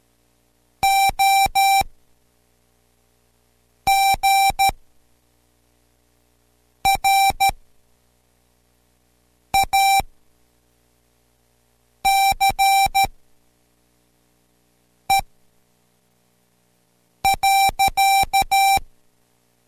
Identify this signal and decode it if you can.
Morse code, heard as a keyed tone, about 13 wpm, spaced as for 4 wpm, 780 Hz. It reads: O GRACE.